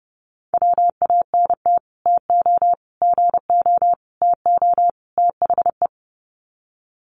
Morse code, heard as a keyed tone, about 30 wpm, 700 Hz. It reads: WANT TO GO TO THE
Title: Sets of 5 Words Encore2 30wpm